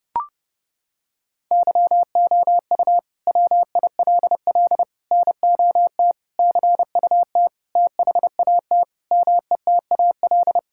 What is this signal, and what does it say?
Morse code, heard as a keyed tone, about 30 wpm, 700 Hz. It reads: YOU WILL NOT CUT THAT METAL
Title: Sentences from Top 1000 Words - Review Part 6 - 30wpm